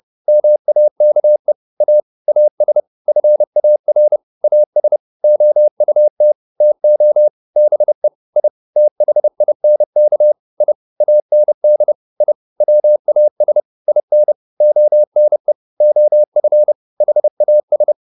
Morse code, heard as a keyed tone, about 30 wpm, 600 Hz. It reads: MAKE A AS FAR AS OUT TO BE I THINK I AND I WAS IN ONE OF HAS